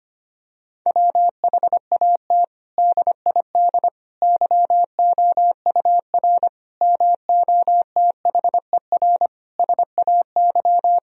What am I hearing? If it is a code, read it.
Morse code, heard as a keyed tone, about 25 wpm, 700 Hz. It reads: WHAT DID YOUR MOTHER SAY
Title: Sentences from Top 1000 Words - Review Part 3 - 25wpm